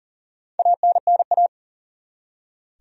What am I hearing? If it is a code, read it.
Morse code, heard as a keyed tone, about 40 wpm, 700 Hz. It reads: ANNA
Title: USA Common Names - Max 5 Letters - 40wpm